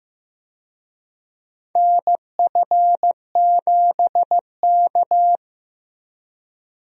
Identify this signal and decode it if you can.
Morse code, heard as a keyed tone, about 15 wpm, 700 Hz. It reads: NF7K
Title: USA Call Signs - 4 Characters - Repeated 2x - 15wpm